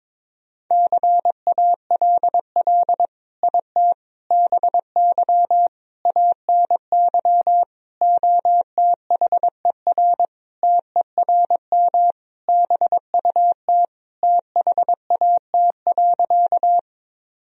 Morse code, heard as a keyed tone, about 22 wpm, 700 Hz. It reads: CALL IT BY ANY OTHER TERM BUT THAT.
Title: Sentences from Top 1000 Words - Review Part 4 - 22wpm